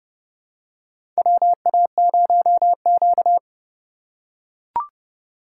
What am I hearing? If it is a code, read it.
Morse code, heard as a keyed tone, about 29 wpm, 700 Hz. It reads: WA0Q
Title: USA Call Signs - 4 Characters - 2x Character Spacing - 30wpm